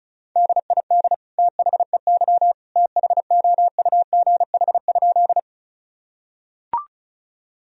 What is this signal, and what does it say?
Morse code, heard as a keyed tone, about 34 wpm, 700 Hz. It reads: DID THEY THOUGH?
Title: Sentences from Top 1000 Words - Review Part 2 - 35wpm